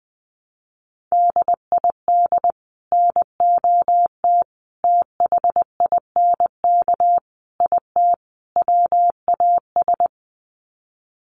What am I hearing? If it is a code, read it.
Morse code, heard as a keyed tone, about 20 wpm, 700 Hz. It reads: DID NOT THINK IT WAS
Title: Sets of 5 Words Encore 20wpm